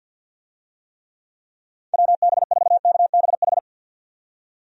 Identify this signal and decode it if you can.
Morse code, heard as a keyed tone, about 49 wpm, 700 Hz. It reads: WB4KBH
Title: USA Call Signs - 6 Characters - 50wpm